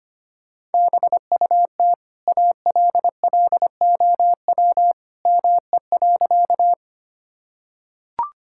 Morse code, heard as a keyed tone, about 25 wpm, 700 Hz. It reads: BUT ALLOW ME.
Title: Sentences from Top 1000 Words - Review Part 5 - 25wpm